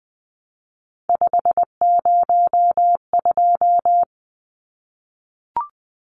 Morse code, heard as a keyed tone, about 20 wpm, 700 Hz. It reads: 502
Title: Numbers - 3-Digit 20wpm